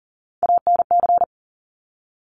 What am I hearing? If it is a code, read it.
Morse code, heard as a keyed tone, about 39 wpm, 700 Hz. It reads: ANC
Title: Top Three Letter Combinations in Words 40wpm